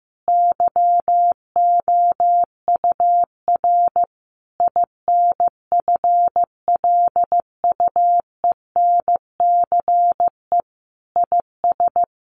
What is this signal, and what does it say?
Morse code, heard as a keyed tone, about 15 wpm, 700 Hz. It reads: YOUR INFLUENCE IS
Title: Sentences from Top 1100 Words 15wpm